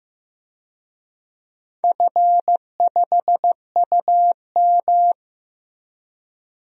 Morse code, heard as a keyed tone, about 16 wpm, 700 Hz. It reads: F5UM
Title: DXCC-100 Call Signs - 4 Characters - Repeated 3x - 15wpm